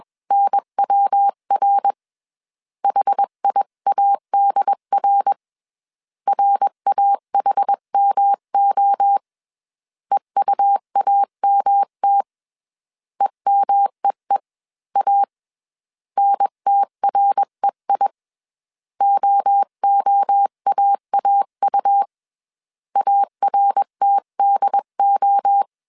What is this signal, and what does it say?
Morse code, heard as a keyed tone, about 21 wpm, 800 Hz. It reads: NWR HIADR RAHMO EUAMT EMEEA NTREI OOAAU ARTDO